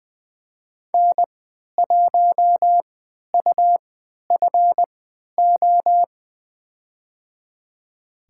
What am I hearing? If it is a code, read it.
Morse code, heard as a keyed tone, about 20 wpm, spaced as for 11 wpm, 700 Hz. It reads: N1UFO
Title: USA Call Signs - 5 Characters - 3x Character Spacing - 20wpm